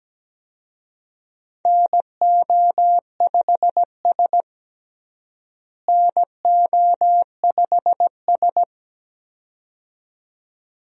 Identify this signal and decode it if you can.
Morse code, heard as a keyed tone, about 17 wpm, 700 Hz. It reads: NO5S NO5S
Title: USA Call Signs - 4 Characters - Repeated 2x - 17wpm